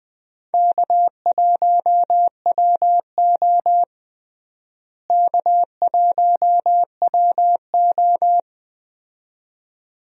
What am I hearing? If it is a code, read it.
Morse code, heard as a keyed tone, about 20 wpm, 700 Hz. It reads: K1WO K1WO
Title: USA Call Signs - 4 Characters - Repeated 2x - 20wpm